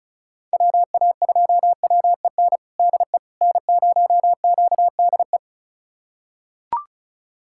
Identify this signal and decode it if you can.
Morse code, heard as a keyed tone, about 34 wpm, 700 Hz. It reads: WA2WEN DE N0QDE
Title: USA Callsign DE Callsign 35wpm